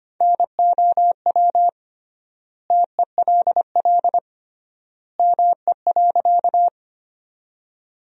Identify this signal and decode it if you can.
Morse code, heard as a keyed tone, about 25 wpm, 700 Hz. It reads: NOW TELL ME.
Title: 3x Word-Spacing - Sentences from Top 200 Words - 25wpm